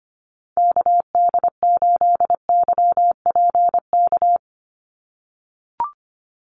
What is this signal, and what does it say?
Morse code, heard as a keyed tone, about 25 wpm, 700 Hz. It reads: KD8YPK
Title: USA Call Signs 25wpm